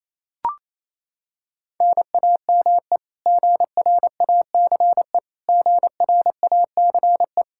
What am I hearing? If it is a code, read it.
Morse code, heard as a keyed tone, about 27 wpm, 700 Hz. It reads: NAME GRACE GRACE
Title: Practice QSO - Exchange 1 - Beginning - 28wpm